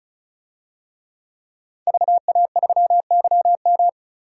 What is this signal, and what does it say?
Morse code, heard as a keyed tone, about 36 wpm, 700 Hz. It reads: VA3YM
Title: DXCC-100 Call Signs - 5 Characters - 35wpm